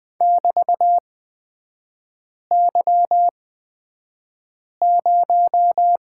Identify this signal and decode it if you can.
Morse code, heard as a keyed tone, about 20 wpm, spaced as for 5 wpm, 700 Hz. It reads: <BT>Y0
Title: 243 Lesson 39e - Letter z - Three Characters 20wpm